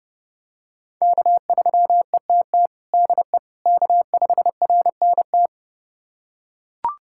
Morse code, heard as a keyed tone, about 30 wpm, 700 Hz. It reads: K3ETT DE K5RNT
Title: USA Callsign DE Callsign - 5 Characters - 30wpm